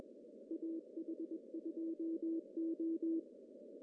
Morse code, heard as a keyed tone, about 21 wpm, 330 Hz. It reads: AH2O
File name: CW / cqww-cw-k5tr